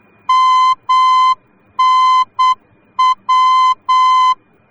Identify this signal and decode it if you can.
Morse code, heard as a keyed tone, about 8 wpm, 1030 Hz. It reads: MNW
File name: germany / mx / MNW-D